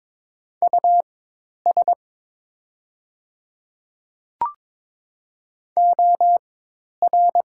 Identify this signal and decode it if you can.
Morse code, heard as a keyed tone, about 22 wpm, spaced as for 10 wpm, 700 Hz. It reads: US OR
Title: 4x Character Spacing - Top 200 Words - 22wpm